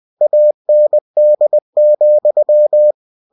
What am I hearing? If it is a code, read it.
Morse code, heard as a keyed tone, about 20 wpm, 600 Hz. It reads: AND,
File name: cw-books / winnie-the-pooh / winnie-20wpm-page-0042